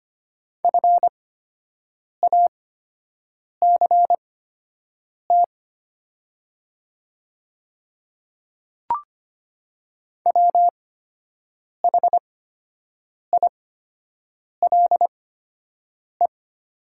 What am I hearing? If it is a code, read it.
Morse code, heard as a keyed tone, about 25 wpm, spaced as for 7 wpm, 700 Hz. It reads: FACT WHILE
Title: 8x Character Spacing - Top 500 Words - 25wpm